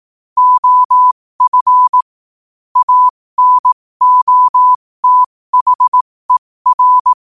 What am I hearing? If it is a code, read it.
Morse code, heard as a keyed tone, about 18 wpm, spaced as for 15 wpm, 1000 Hz. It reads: OF ANOTHER
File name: qotdfiles / 15wpm